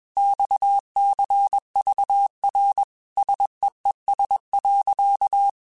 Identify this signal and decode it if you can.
Morse code, heard as a keyed tone, about 21 wpm, 800 Hz. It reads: XCVR SEES.
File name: QSOs / QSOs-mp3s / WD0FBT 1@20